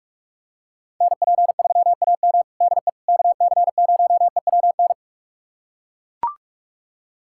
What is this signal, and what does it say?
Morse code, heard as a keyed tone, about 42 wpm, 690 Hz. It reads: NP3AM DE KC0EWN
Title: USA Callsign DE Callsign 45wpm